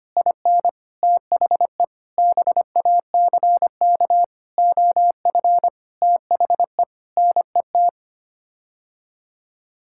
Morse code, heard as a keyed tone, about 25 wpm, 700 Hz. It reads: IN THE BACK OF THE NET
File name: Sets of 9 Words - 25wpm